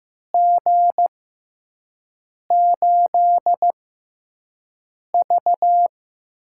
Morse code, heard as a keyed tone, about 15 wpm, spaced as for 5 wpm, 700 Hz. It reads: G8V
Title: USA Call Signs - 4 Characters - 6x Character Spacing - 15wpm